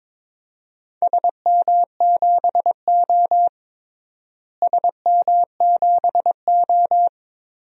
Morse code, heard as a keyed tone, about 22 wpm, 700 Hz. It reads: SM7O SM7O
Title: DXCC-100 Call Signs - 4 Characters - Repeated 3x - 22wpm